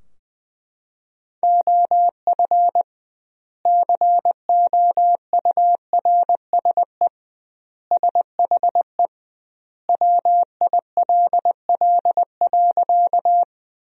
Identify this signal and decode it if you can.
Morse code, heard as a keyed tone, about 20 wpm, 700 Hz. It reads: OF COURSE SHE WILL.
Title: 2x Word-Spacing - Sentences from Top 500 Words - 20wpm